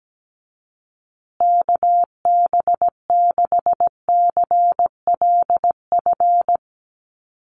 Morse code, heard as a keyed tone, about 17 wpm, 700 Hz. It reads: KB6CLF
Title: USA Call Signs Repeated 3x 17wpm